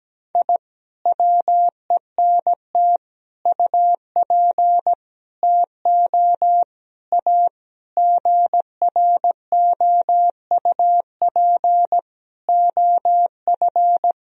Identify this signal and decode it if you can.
Morse code, heard as a keyed tone, about 17 wpm, 700 Hz. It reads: I WENT UP TO A GROUP OF
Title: Sentences from Top 1000 Words - Review Part 1 - 17wpm